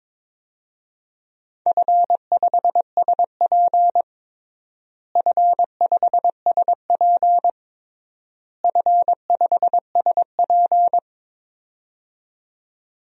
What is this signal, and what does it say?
Morse code, heard as a keyed tone, about 22 wpm, 700 Hz. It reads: F5SP F5SP F5SP
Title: DXCC-100 Call Signs - 4 Characters - Repeated 3x - 22wpm